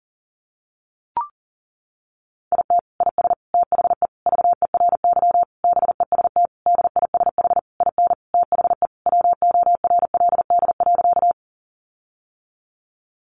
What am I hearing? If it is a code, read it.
Morse code, heard as a keyed tone, about 40 wpm, 700 Hz. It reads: IT IS THE VERY BEST DISH IN THE WORLD.
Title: Sentences from Top 1300 Words 40wpm